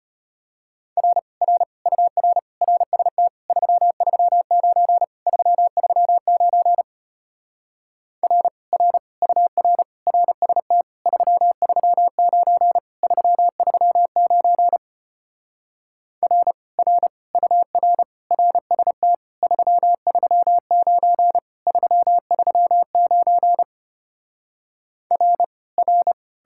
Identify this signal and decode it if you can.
Morse code, heard as a keyed tone, about 33 wpm, 700 Hz. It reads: R R UR RST 339 339 R R UR RST 339 339 R R UR RST 339 339 R R